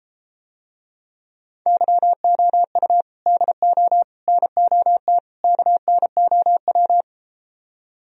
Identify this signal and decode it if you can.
Morse code, heard as a keyed tone, about 33 wpm, 700 Hz. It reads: YOU DO NOT KNOW